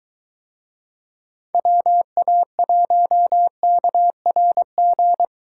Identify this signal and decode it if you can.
Morse code, heard as a keyed tone, about 21 wpm, 700 Hz. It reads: WA1KRG